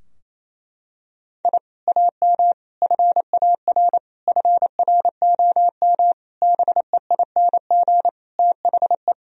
Morse code, heard as a keyed tone, about 28 wpm, 700 Hz. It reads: I AM FAR FROM BEING THE
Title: Sentences from Top 1000 Words - Review Part 3 - 28wpm